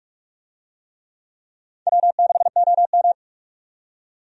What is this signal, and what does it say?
Morse code, heard as a keyed tone, about 46 wpm, 700 Hz. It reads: W6OM
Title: USA Call Signs - 4 Characters - 45wpm